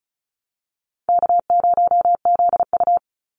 Morse code, heard as a keyed tone, about 34 wpm, 700 Hz. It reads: K0ZU
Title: Easy Call Signs Repeated 3x - 35wpm